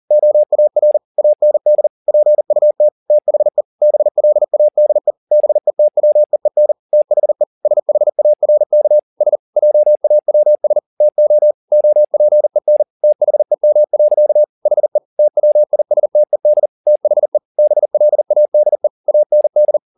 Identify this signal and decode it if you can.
Morse code, heard as a keyed tone, about 40 wpm, 600 Hz. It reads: OAR AND PUT THE BLADE BETWEEN THE SHARK S JAWS TO OPEN THEM. HE TWISTED THE BLADE AND
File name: the old man and the sea 2 40wpm